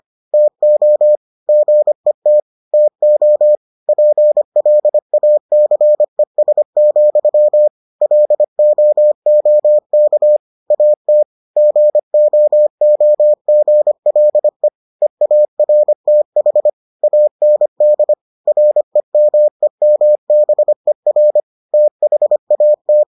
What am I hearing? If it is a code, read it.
Morse code, heard as a keyed tone, about 25 wpm, 600 Hz. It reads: TO GET TO PLACES, LOOK AT GOOGLE EARTH AND REMEMBER THAT